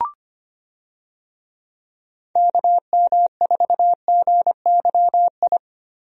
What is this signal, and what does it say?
Morse code, heard as a keyed tone, about 25 wpm, 700 Hz. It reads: KM4GYI
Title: USA Call Signs - 6 Characters - Repeated 2x - 25wpm